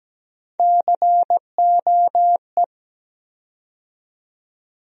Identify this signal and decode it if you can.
Morse code, heard as a keyed tone, about 17 wpm, 700 Hz. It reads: COE
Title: USA Call Sign Postfixes - 3 Letters 17wpm